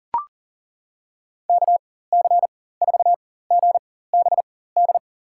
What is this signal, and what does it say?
Morse code, heard as a keyed tone, about 40 wpm, spaced as for 19 wpm, 700 Hz. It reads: KC4GBD
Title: USA Call Signs - 6 Characters - 4x Character Spacing - 40wpm